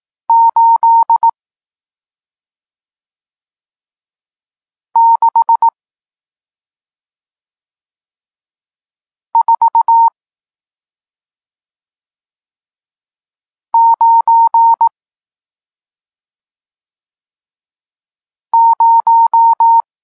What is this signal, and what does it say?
Morse code, heard as a keyed tone, about 18 wpm, 920 Hz. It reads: 8 6 4 9 0
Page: Tekst 40 – Nauka znaków przestankowych